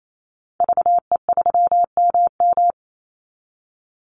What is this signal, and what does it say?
Morse code, heard as a keyed tone, about 28 wpm, 700 Hz. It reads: VE3MM
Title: Call Signs Repeated 2x - 28wpm